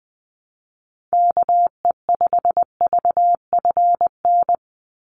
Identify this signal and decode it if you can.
Morse code, heard as a keyed tone, about 20 wpm, 700 Hz. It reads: KE5VFN
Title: USA Call Signs 20wpm